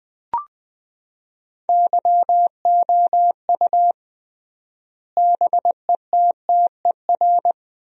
Morse code, heard as a keyed tone, about 20 wpm, 700 Hz. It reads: YOU BETTER
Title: 3x Word-Spacing - Sentences from Top 500 Words - 20wpm